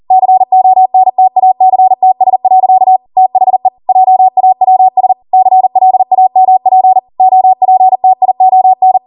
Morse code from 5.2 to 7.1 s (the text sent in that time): CLAMP